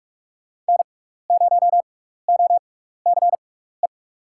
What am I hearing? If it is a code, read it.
Morse code, heard as a keyed tone, about 42 wpm, spaced as for 15 wpm, 700 Hz. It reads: N0OCE
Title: USA Call Signs - 5 Characters - 6x Character Spacing - 45wpm